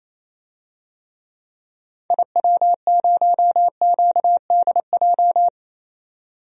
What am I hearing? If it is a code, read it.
Morse code, heard as a keyed tone, about 28 wpm, 700 Hz. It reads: IW0QDJ
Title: DXCC-100 Call Signs - 6 Characters - 28wpm